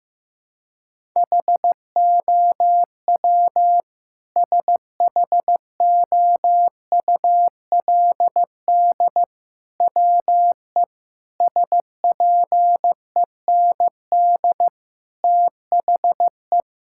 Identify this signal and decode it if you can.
Morse code, heard as a keyed tone, about 16 wpm, 700 Hz. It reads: HOW SHOULD WE SPEND THE
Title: Sentences from Top 1000 Words - Review Part 5 - 15wpm